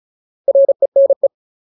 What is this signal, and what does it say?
Morse code, heard as a keyed tone, about 34 wpm, 550 Hz. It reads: RENE
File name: Ultimate Word List / 35WPM Ultimate